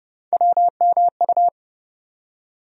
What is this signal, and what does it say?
Morse code, heard as a keyed tone, about 30 wpm, 700 Hz. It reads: WMU
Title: USA Call Sign Postfixes - 3 Letters 30wpm